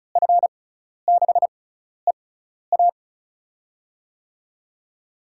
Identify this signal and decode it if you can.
Morse code, heard as a keyed tone, about 36 wpm, spaced as for 12 wpm, 700 Hz. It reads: F6EA
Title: USA Call Signs - 5 Characters - 6x Character Spacing - 35wpm